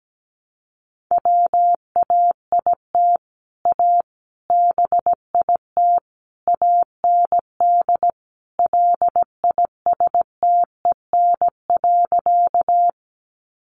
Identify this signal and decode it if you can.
Morse code, heard as a keyed tone, about 17 wpm, 700 Hz. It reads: WAIT A BIT AND LISTEN.
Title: Sentences from Top 900 Words 17wpm